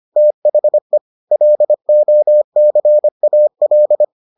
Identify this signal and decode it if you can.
Morse code, heard as a keyed tone, about 25 wpm, 600 Hz. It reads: THE LOCAL